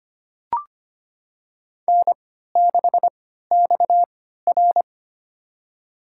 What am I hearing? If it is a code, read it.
Morse code, heard as a keyed tone, about 25 wpm, spaced as for 14 wpm, 700 Hz. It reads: N6XR
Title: USA Call Signs - 4 Characters - 3x Character Spacing - 25wpm